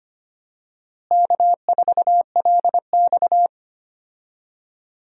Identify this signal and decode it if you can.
Morse code, heard as a keyed tone, about 25 wpm, 700 Hz. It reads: K4LX
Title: USA Call Signs - 4 Characters - Repeated 3x - 25wpm